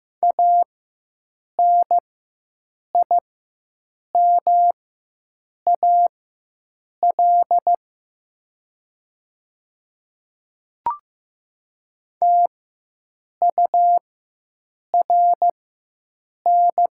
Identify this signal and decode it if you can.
Morse code, heard as a keyed tone, about 15 wpm, spaced as for 7 wpm, 700 Hz. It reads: ANIMAL TURN
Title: 4x Character Spacing - Top 500 Words - 15wpm